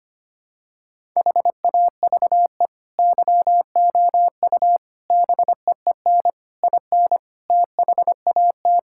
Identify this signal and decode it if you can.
Morse code, heard as a keyed tone, about 25 wpm, 700 Hz. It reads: HAVE YOU BEEN IN THAT
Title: Sentences from Top 1000 Words - Review Part 5 - 25wpm